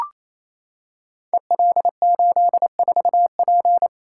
Morse code, heard as a keyed tone, about 29 wpm, 700 Hz. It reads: EL84P
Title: Code Groups - 5 Letter:Numbers - 28wpm